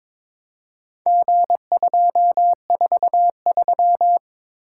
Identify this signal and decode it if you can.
Morse code, heard as a keyed tone, about 22 wpm, 700 Hz. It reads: G243
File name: practice / 22wpm / Four Letter-Number Combinations - 22wpm